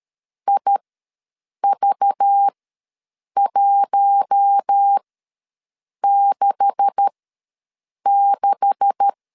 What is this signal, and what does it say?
Morse code, heard as a keyed tone, about 13 wpm, spaced as for 7 wpm, 800 Hz. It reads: IV166
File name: LEZIONI / 9 286 ws 12wpm 3351